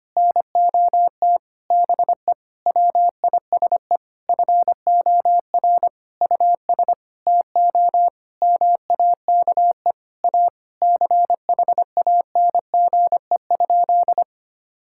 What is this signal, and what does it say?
Morse code, heard as a keyed tone, about 25 wpm, 700 Hz. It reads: NOT BE WISE FOR US TO MAKE A CHANGE?
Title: Sentences from Top 1600 Words 25wpm